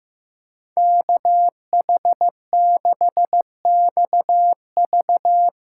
Morse code, heard as a keyed tone, about 15 wpm, 700 Hz. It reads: KH6XV
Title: USA Callsign DE Callsign - 5 Characters - 15wpm